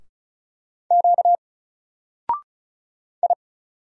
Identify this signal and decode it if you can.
Morse code, heard as a keyed tone, about 35 wpm, 700 Hz. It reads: Q I
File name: Single-Letter-Number-35wpm